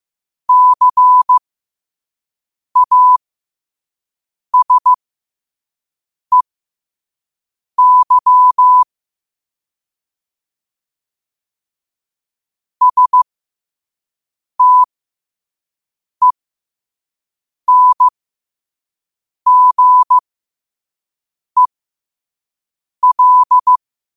Morse code, heard as a keyed tone, about 15 wpm, spaced as for 5 wpm, 1000 Hz. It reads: CASEY STENGEL
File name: 5wpm